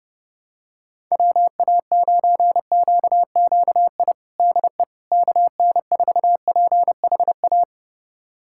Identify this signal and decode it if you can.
Morse code, heard as a keyed tone, about 30 wpm, 700 Hz. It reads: WA9QQI DE KN4PHA
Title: USA Callsign DE Callsign 30wpm